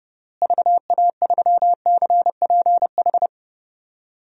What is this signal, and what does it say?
Morse code, heard as a keyed tone, about 30 wpm, 700 Hz. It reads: VA3CPH
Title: DXCC-100 Call Signs - 6 Characters - Repeated 3x - 30wpm